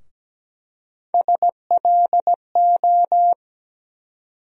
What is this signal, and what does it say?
Morse code, heard as a keyed tone, about 17 wpm, 700 Hz. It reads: SLO
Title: USA Call Sign Postfixes - 3 Letters 17wpm